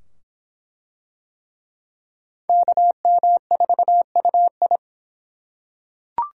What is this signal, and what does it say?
Morse code, heard as a keyed tone, about 24 wpm, 700 Hz. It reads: KM4UI